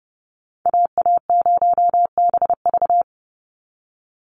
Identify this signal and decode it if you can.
Morse code, heard as a keyed tone, about 30 wpm, 700 Hz. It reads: AA0BV
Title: NA SOTA Activator Callsigns 30wpm